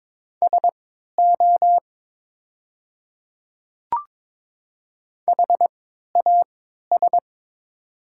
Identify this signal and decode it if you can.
Morse code, heard as a keyed tone, about 22 wpm, spaced as for 12 wpm, 700 Hz. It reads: SO HAS
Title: 3x Character Spacing - Top 100 Words - 22wpm